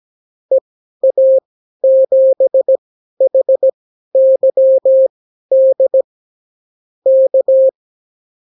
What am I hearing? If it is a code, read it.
Morse code, heard as a keyed tone, about 17 wpm, spaced as for 12 wpm, 530 Hz. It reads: EA7HYD K
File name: CW QSO libroCW4 18-8ppm